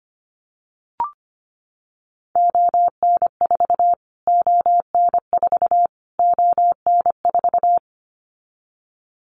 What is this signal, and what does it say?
Morse code, heard as a keyed tone, about 25 wpm, 700 Hz. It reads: ON4 ON4 ON4
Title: Callsign Prefixes for Top 25 DXCC - Repeated 3x - 25wpm